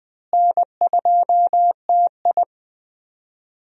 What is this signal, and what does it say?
Morse code, heard as a keyed tone, about 20 wpm, 700 Hz. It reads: N2TI
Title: USA Call Signs - 4 Characters - Repeated 2x - 20wpm